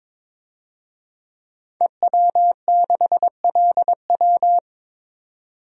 Morse code, heard as a keyed tone, about 22 wpm, 700 Hz. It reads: EW6LW
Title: DXCC-100 Call Signs - 5 Characters - 22wpm